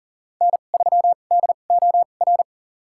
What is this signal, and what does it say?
Morse code, heard as a keyed tone, about 40 wpm, spaced as for 29 wpm, 700 Hz. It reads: N3DOR
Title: USA Call Signs - 5 Characters - 2x Character Spacing - 40wpm